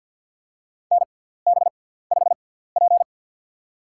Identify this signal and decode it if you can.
Morse code, heard as a keyed tone, about 50 wpm, spaced as for 17 wpm, 690 Hz. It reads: NB5P
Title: USA Call Signs - 6x Character Spacing - 50wpm